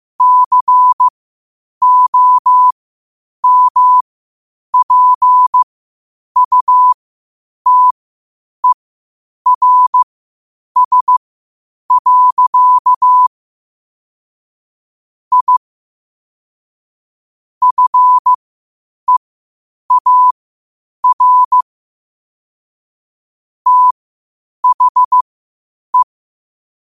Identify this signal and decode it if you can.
Morse code, heard as a keyed tone, about 15 wpm, spaced as for 8 wpm, 1000 Hz. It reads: COMPUTERS. I FEAR THE
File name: qotdfiles / 8wpm